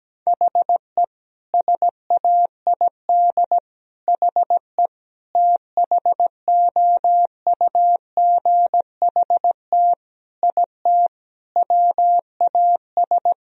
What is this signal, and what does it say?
Morse code, heard as a keyed tone, about 17 wpm, 700 Hz. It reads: HE SAID HE THOUGHT IT WAS
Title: Sentences from Top 1000 Words - Review Part 3 - 17wpm